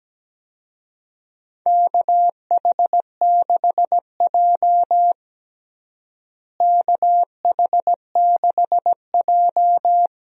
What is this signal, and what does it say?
Morse code, heard as a keyed tone, about 17 wpm, 700 Hz. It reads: KH6J KH6J
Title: USA Call Signs - 4 Characters - Repeated 3x - 17wpm